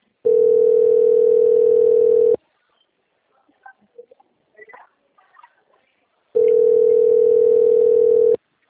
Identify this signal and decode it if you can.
Ringback tone (North America), 440 + 480 Hz.